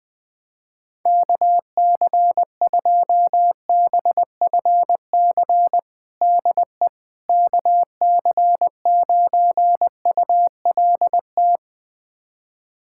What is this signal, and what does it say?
Morse code, heard as a keyed tone, about 20 wpm, 700 Hz. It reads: KC2BFC DE KC9ULT
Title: USA Callsign DE Callsign - 6 Characters - 20wpm